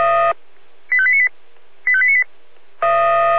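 Five-tone selective call (ZVEI), heard as quick runs of five tones, tones of about 70 ms. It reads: 85898 85898
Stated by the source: Five-tone ZVEI selective call received by SDR, from the fire brigade Feuerwehr Orlamünde audio server.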